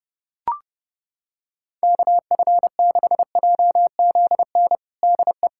Morse code, heard as a keyed tone, about 28 wpm, 700 Hz. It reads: KF6JZN DE
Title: USA Callsign DE Callsign - 6 Characters - 30wpm